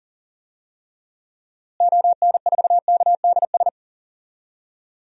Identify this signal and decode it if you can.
Morse code, heard as a keyed tone, about 40 wpm, 700 Hz. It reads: ON4KDS
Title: DXCC-100 Call Signs - 6 Characters - Repeated 3x - 40wpm